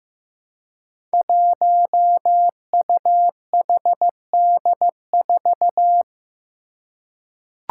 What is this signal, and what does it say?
Morse code, heard as a keyed tone, about 15 wpm, 700 Hz. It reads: 1UHD4
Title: Five Challenging Letter-Number Combinations 15wpm